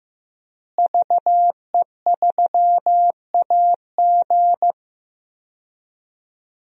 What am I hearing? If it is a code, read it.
Morse code, heard as a keyed tone, about 15 wpm, 700 Hz. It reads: VE3AG
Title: DXCC-100 Call Signs - 5 Characters - Repeated 2x - 15wpm